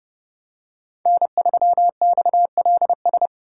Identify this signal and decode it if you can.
Morse code, heard as a keyed tone, about 30 wpm, 700 Hz. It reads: N3XLS
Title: K1USN SST QSO Practice - 30wpm